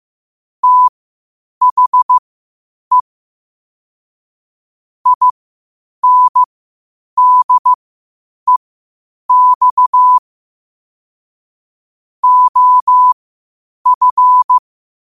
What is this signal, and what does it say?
Morse code, heard as a keyed tone, about 15 wpm, spaced as for 8 wpm, 1000 Hz. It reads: THE INDEX OF
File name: qotdfiles / 8wpm